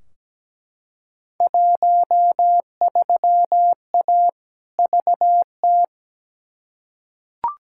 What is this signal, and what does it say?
Morse code, heard as a keyed tone, about 17 wpm, 700 Hz. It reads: 13A VT
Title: Field Day Exchanges Encore - 17wpm